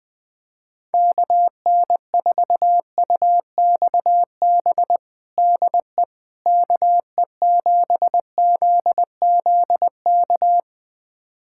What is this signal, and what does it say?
Morse code, heard as a keyed tone, about 20 wpm, 700 Hz. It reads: KN4UXB DE KE7ZZK
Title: USA Callsign DE Callsign - 6 Characters - 20wpm